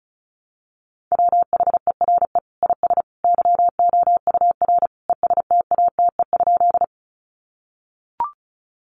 Morse code, heard as a keyed tone, about 34 wpm, 700 Hz. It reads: WHERE IS YOUR ESTATE?
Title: Sentences from Top 1200 Words 35wpm